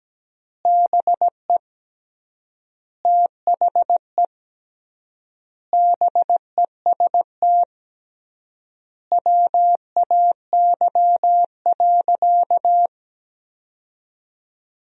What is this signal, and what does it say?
Morse code, heard as a keyed tone, about 17 wpm, 700 Hz. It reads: BE THE BEST WAY.
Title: 3x Word-Spacing - Sentences from Top 400 Words - 17wpm